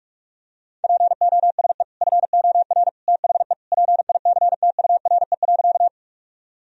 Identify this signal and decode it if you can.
Morse code, heard as a keyed tone, about 46 wpm, 700 Hz. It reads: POSE FOR THE PICTURE.
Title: Sentences from Top 1000 Words - Review Part 2 - 45wpm